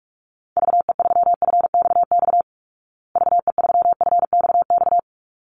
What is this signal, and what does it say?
Morse code, heard as a keyed tone, about 44 wpm, 700 Hz. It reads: VE3FXX VE3FXX
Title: Call Signs Repeated 3x - 45wpm